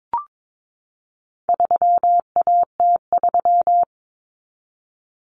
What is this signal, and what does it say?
Morse code, heard as a keyed tone, about 22 wpm, 700 Hz. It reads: 3AT3
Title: Common Cut Numbers - 4-Digit 22wpm